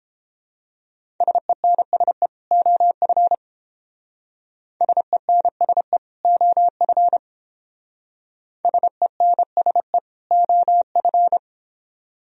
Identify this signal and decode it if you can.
Morse code, heard as a keyed tone, about 29 wpm, 700 Hz. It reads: SENSE OF SENSE OF SENSE OF